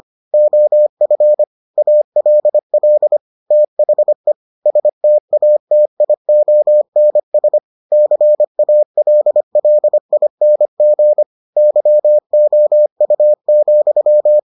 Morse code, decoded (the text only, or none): OF ALL THE STATIONS CALLING YOU,